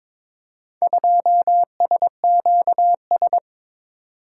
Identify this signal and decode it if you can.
Morse code, heard as a keyed tone, about 22 wpm, 700 Hz. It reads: 2SQS
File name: practice / 22wpm / Four Letter-Number Combinations - 22wpm